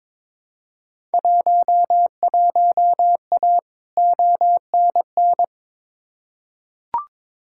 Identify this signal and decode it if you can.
Morse code, heard as a keyed tone, about 21 wpm, 700 Hz. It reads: 11A ONN
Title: Field Day Exchanges Encore - 22wpm